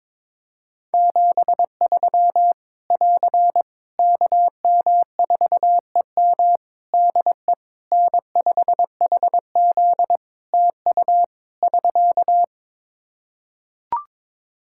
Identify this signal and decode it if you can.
Morse code, heard as a keyed tone, about 22 wpm, 700 Hz. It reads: 73 <AR> KM4EM DE N5HZ TU <SK>
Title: Practice QSO - Exchange 4 - Ending - 22wpm